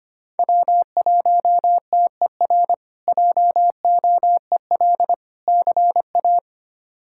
Morse code, heard as a keyed tone, about 25 wpm, 700 Hz. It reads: W1TER JOEL CA
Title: K1USN SST QSO Practice - 25wpm